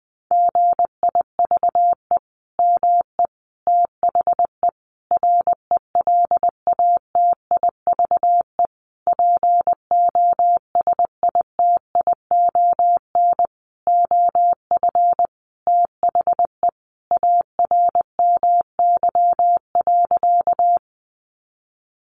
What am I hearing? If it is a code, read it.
Morse code, heard as a keyed tone, about 19 wpm, 700 Hz. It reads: GIVE ME THE RELATIVE POSITION OF THE ARMY.
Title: Sentences from Top 1400 Words 20wpm